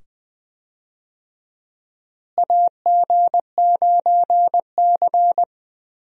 Morse code, heard as a keyed tone, about 20 wpm, 700 Hz. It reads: AG9C
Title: USA Call Signs - 4 Characters - Repeated 2x - 20wpm